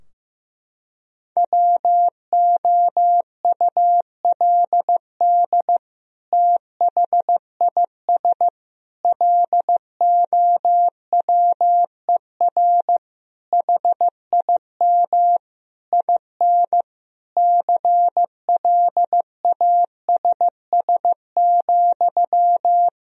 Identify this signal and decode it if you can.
Morse code, heard as a keyed tone, about 15 wpm, 700 Hz. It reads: WOULD THIS LOWER HIM IN CLASS,
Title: Sentences from Top 1000 Words - Review Part 4 - 15wpm